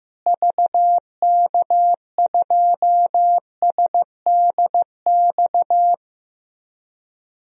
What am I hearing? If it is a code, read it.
Morse code, heard as a keyed tone, about 15 wpm, 700 Hz. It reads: VK2SDX
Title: DXCC-100 Call Signs - 6 Characters - Repeated 3x - 15wpm